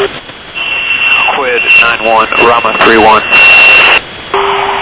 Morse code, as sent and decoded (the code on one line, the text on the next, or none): none